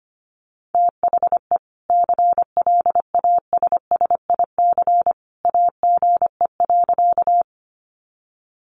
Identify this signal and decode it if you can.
Morse code, heard as a keyed tone, about 25 wpm, 700 Hz. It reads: THE CLASSIC AGE.
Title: Sentences from Top 1000 Words 25wpm